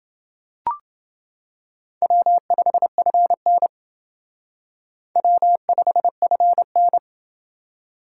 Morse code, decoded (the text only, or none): W5FN W5FN